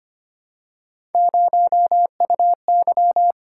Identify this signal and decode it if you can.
Morse code, heard as a keyed tone, about 25 wpm, 700 Hz. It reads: 0UY
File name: Three Letter-Number Combinations - 25wpm